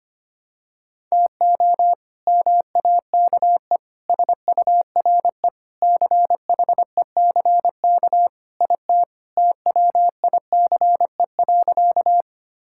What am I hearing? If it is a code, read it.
Morse code, heard as a keyed tone, about 25 wpm, 700 Hz. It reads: TO MAKE SURE CHECK IT TWICE.
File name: practice / 25wpm / Sentences with 3-Word Phrases - 25wpm